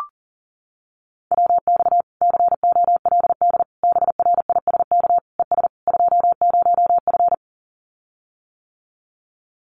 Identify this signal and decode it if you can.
Morse code, heard as a keyed tone, about 40 wpm, 700 Hz. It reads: WX COLD BRISK ES 20F
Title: Common QSO Phrases Encore 40wpm